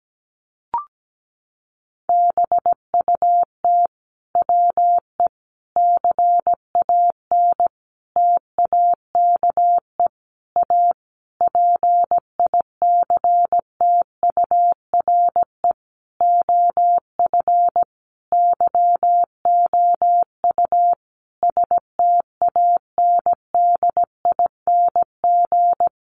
Morse code, heard as a keyed tone, about 17 wpm, 700 Hz. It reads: BUT WE CAN TAKE A PICTURE OF YOU STANDING
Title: A Sound Of Thunder 17wpm